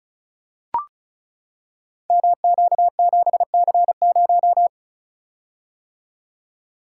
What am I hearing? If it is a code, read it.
Morse code, heard as a keyed tone, about 36 wpm, 690 Hz. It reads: MQ7C0
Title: Code Groups - 5 Letter:Numbers - 35wpm